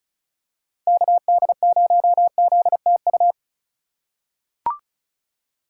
Morse code, heard as a keyed tone, about 36 wpm, 700 Hz. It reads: KD0ZTU
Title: USA Call Signs - 3x Character Spacing - 35wpm